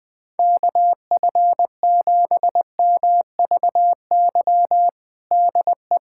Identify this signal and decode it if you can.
Morse code, heard as a keyed tone, about 20 wpm, 700 Hz. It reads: KF7MVY DE
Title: USA Callsign DE Callsign - 6 Characters - 20wpm